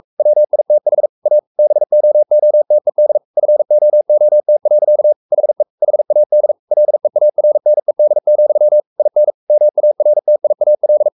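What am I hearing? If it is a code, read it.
Morse code, heard as a keyed tone, about 44 wpm, 600 Hz. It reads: WITH A BOOTED FOOT. HE HAD LEARNED, IN MARTIAL
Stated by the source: Morse code at 43 wpm sending Ken Follett's novel Never in English, from CW med Gnister.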